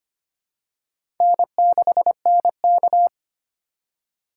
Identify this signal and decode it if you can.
Morse code, heard as a keyed tone, about 25 wpm, 700 Hz. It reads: N6NK
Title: USA Call Signs - 4 Characters - 6x Character Spacing - 25wpm